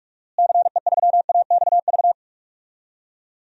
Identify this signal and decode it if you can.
Morse code, heard as a keyed tone, about 46 wpm, 700 Hz. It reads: CE3AXV
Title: DXCC-100 Call Signs - 6 Characters - Repeated 2x - 45wpm